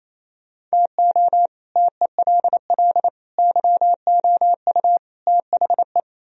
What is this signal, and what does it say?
Morse code, heard as a keyed tone, about 37 wpm, 700 Hz. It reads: TO TELL YOU THE